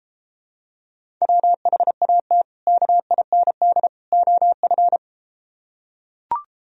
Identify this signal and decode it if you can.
Morse code, heard as a keyed tone, about 33 wpm, 700 Hz. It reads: WHAT KIND OF